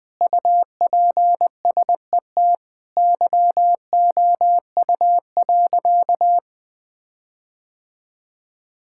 Morse code, heard as a keyed tone, about 20 wpm, 700 Hz. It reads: UPSET YOU.
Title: Sentences with 5-Word Phrases - 20wpm